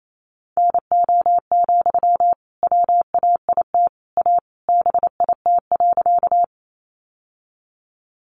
Morse code, heard as a keyed tone, about 28 wpm, 700 Hz. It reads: NO, WAIT A BIT.
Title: Sentences from Top 900 Words 28wpm